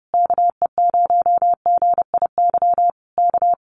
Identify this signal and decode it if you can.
Morse code, heard as a keyed tone, about 30 wpm, 700 Hz. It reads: KE0GIY K
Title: Common QSO Phrases Encore 30wpm